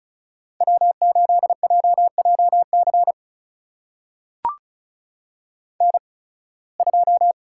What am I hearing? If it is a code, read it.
Morse code, heard as a keyed tone, about 36 wpm, spaced as for 23 wpm, 700 Hz. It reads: W8JJC N2
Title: USA Call Signs - 5 Characters - 8x Character Spacing - 35wpm